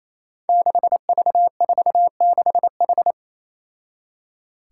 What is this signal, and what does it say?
Morse code, heard as a keyed tone, about 28 wpm, 700 Hz. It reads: 6V46H
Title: Five Challenging Letter-Number Combinations 28wpm